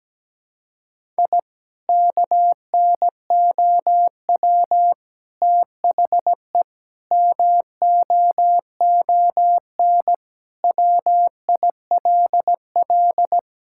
Morse code, heard as a keyed tone, about 17 wpm, 700 Hz. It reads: I KNOW THE MOON WILL
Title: Sentences from Top 1000 Words - Review Part 6 - 17wpm